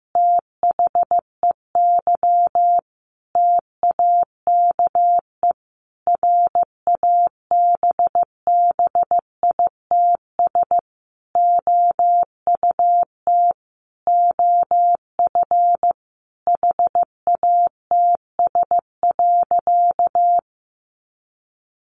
Morse code, heard as a keyed tone, about 15 wpm, 700 Hz. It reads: THEY TAKE RABBITS OUT OF HATS.
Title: Easy Phrases 15wpm